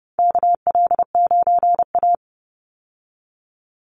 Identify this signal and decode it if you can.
Morse code, heard as a keyed tone, about 28 wpm, 700 Hz. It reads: KL9A
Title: Call Signs Repeated 3x - 30wpm